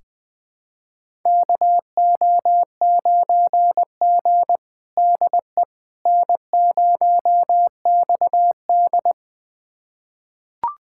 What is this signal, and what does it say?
Morse code, heard as a keyed tone, about 20 wpm, 700 Hz. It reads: KO9G DE N0XD
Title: USA Callsign DE Callsign - 4 Characters - 20wpm